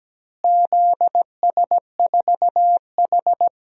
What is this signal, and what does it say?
Morse code, heard as a keyed tone, about 17 wpm, 700 Hz. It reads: ZS4H
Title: DXCC-100 Call Signs - 4 Characters - Repeated 3x - 17wpm